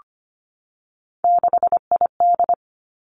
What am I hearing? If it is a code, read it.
Morse code, heard as a keyed tone, about 25 wpm, 700 Hz. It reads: 6ID
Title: Three Challenging Letter-Number Combinations 25wpm